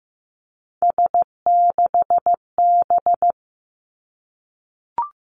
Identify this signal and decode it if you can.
Morse code, heard as a keyed tone, about 15 wpm, 700 Hz. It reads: S6B
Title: Three Challenging Letter-Number Combinations 15wpm